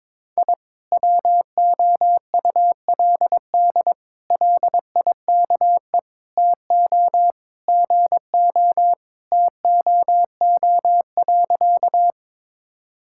Morse code, heard as a keyed tone, about 22 wpm, 700 Hz. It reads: I WOULD LIKE TO GO TOO.
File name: practice / 22wpm / Sentences with 3-Word Phrases - 22wpm